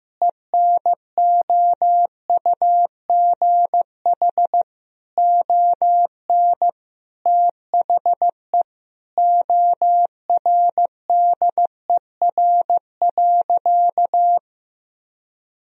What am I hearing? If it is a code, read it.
Morse code, heard as a keyed tone, about 15 wpm, 700 Hz. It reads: ENOUGH ON THE ORDER.
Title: Sentences from Top 1000 Words - Review Part 2 - 15wpm